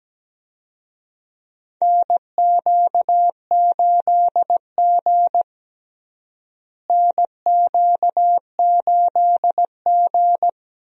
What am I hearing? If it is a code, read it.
Morse code, heard as a keyed tone, about 17 wpm, 700 Hz. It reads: NQ8G NQ8G
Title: USA Call Signs - 4 Characters - Repeated 2x - 17wpm